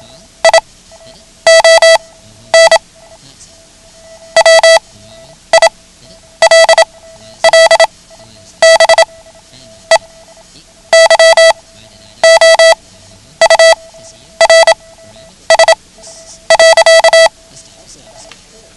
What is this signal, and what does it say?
Morse code, heard as a keyed tone, about 27 wpm, spaced as for 10 wpm, 660 Hz. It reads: ION WILL BE YOURS.
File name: ham radio / CodeQuick / Code Quick 09